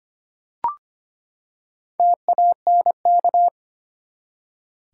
Morse code, heard as a keyed tone, about 25 wpm, 700 Hz. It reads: TANK
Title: Four Letter Words - 25wpm